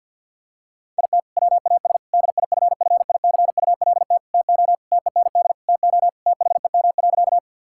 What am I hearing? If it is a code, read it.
Morse code, heard as a keyed tone, about 51 wpm, 700 Hz. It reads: IT WAS DIFFICULT TO TEND TO THEM.
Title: Sentences from Top 1000 Words - Review Part 5 - 50wpm